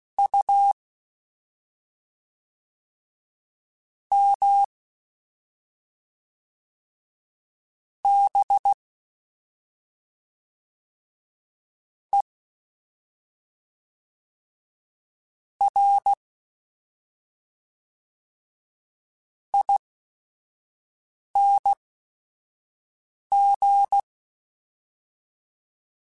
Morse code, heard as a keyed tone, about 16 wpm, spaced as for 5 wpm, 800 Hz. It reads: U M B E R ING
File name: Special INGs / Special INGs-mp3s / SlowToFaster ing @5